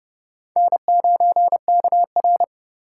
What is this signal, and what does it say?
Morse code, heard as a keyed tone, about 30 wpm, 700 Hz. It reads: N9KR
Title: K1USN SST QSO Practice - 30wpm